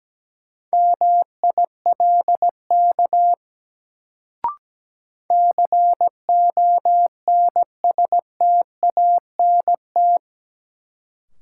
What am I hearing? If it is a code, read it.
Morse code, heard as a keyed tone, about 17 wpm, 700 Hz. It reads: MILK CONSTANT
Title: Sentences from Top 1500 Words - New Word Drill Alt - 17wpm